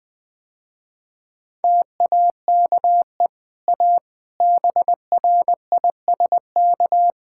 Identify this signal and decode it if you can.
Morse code, heard as a keyed tone, about 20 wpm, 700 Hz. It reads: TAKE A BRISK
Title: Easy Sentences with Hints - 20wpm